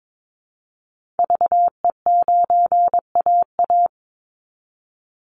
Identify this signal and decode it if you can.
Morse code, heard as a keyed tone, about 22 wpm, 700 Hz. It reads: VE9AA
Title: Call Signs Repeated 3x - 22wpm